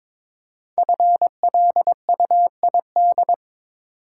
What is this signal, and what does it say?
Morse code, heard as a keyed tone, about 22 wpm, 700 Hz. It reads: FLUID
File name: 5 Letter Words - First 3 Letters Given - 22wpm